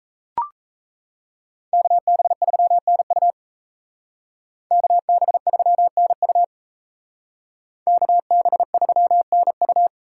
Morse code, heard as a keyed tone, about 36 wpm, 700 Hz. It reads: KB3NU KB3NU KB3NU